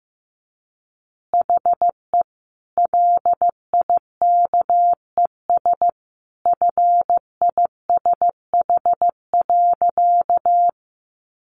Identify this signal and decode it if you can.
Morse code, heard as a keyed tone, about 15 wpm, 700 Hz. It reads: HE LIKES FISH.
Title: Additional Easy Phrases 15wpm